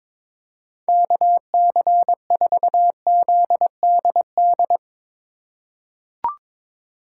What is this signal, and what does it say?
Morse code, heard as a keyed tone, about 22 wpm, 700 Hz. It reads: KC4ZDD
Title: USA Call Signs - 6 Characters - 4x Character Spacing - 22wpm